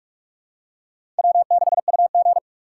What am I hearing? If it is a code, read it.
Morse code, heard as a keyed tone, about 43 wpm, 700 Hz. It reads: W6UG
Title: USA Call Signs - 4x Character Spacing - 45wpm